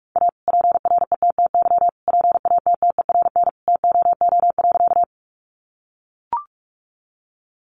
Morse code, heard as a keyed tone, about 44 wpm, 700 Hz. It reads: A PRETTY PATTERN TOO.
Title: Sentences from Top 1000 Words 45wpm